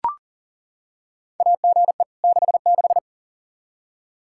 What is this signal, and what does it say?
Morse code, heard as a keyed tone, about 37 wpm, 700 Hz. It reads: AGE 66
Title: Practice QSO - Exchange 3 - More Info - 40wpm